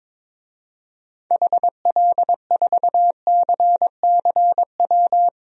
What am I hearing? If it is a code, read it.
Morse code, heard as a keyed tone, about 22 wpm, 700 Hz. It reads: HL4CCW
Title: DXCC-100 Call Signs - 6 Characters - Repeated 3x - 22wpm